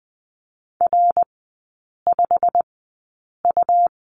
Morse code, heard as a keyed tone, about 20 wpm, spaced as for 8 wpm, 700 Hz. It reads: R5U
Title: 100 Lesson 17e - Letter u - Three Characters 20wpm